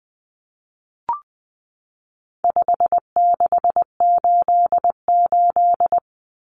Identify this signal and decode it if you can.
Morse code, heard as a keyed tone, about 20 wpm, 700 Hz. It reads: 5688
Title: Common Cut Numbers - 4-Digit 20wpm